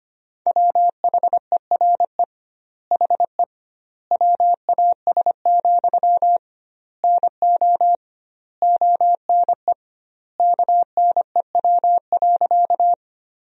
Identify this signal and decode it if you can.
Morse code, heard as a keyed tone, about 25 wpm, 700 Hz. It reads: WHERE HE WAS, NO ONE KNEW.
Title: 2x Word-Spacing - Sentences from Top 400 Words - 25wpm